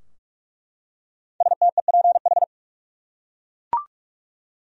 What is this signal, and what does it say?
Morse code, heard as a keyed tone, about 46 wpm, 700 Hz. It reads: STEPH
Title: USA Common Names - Max 5 Letters - 45wpm